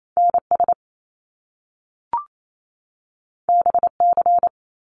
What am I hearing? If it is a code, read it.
Morse code, heard as a keyed tone, about 28 wpm, 700 Hz. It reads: NS BC
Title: Canadian Provinces and Territories 28wpm